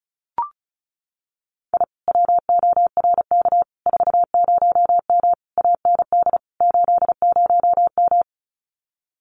Morse code, heard as a keyed tone, about 36 wpm, 700 Hz. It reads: I WORK 40M AND 80M
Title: Common QSO Phrases Encore 35wpm